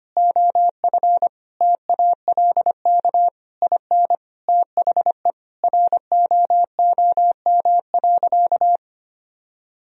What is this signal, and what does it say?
Morse code, heard as a keyed tone, about 25 wpm, 700 Hz. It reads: OF TALK IN THE ROOM.
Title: Sentences from Top 1000 Words - Review Part 4 - 25wpm